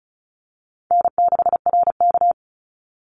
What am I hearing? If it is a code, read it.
Morse code, heard as a keyed tone, about 36 wpm, 700 Hz. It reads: N6RK
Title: Call Signs - 35wpm